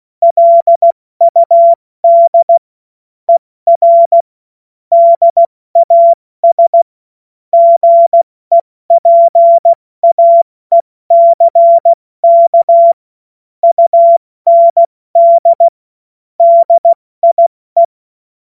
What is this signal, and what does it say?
Morse code, heard as a keyed tone, about 16 wpm, 680 Hz. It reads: LUD ER DAS GEPAECK UND DIE
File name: Morsekurs / Witze / Witz 16 Januar 2024 70 F80